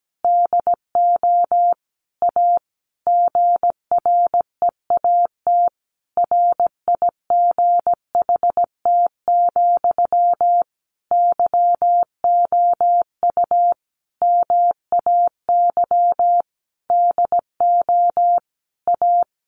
Morse code, heard as a keyed tone, about 17 wpm, 700 Hz. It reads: DO A GREAT RIGHT, YOU MAY DO A
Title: Sentences from Top 900 Words 17wpm